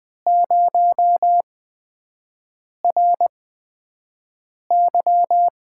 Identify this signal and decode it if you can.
Morse code, heard as a keyed tone, about 20 wpm, spaced as for 5 wpm, 700 Hz. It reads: 0RY
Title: USA Call Signs - 4 Characters - 8x Character Spacing - 20wpm